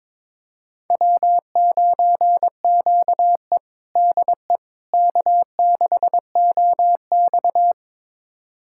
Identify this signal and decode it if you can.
Morse code, heard as a keyed tone, about 22 wpm, 700 Hz. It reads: W9QE DE K6OX
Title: USA Callsign DE Callsign - 4 Characters - 22wpm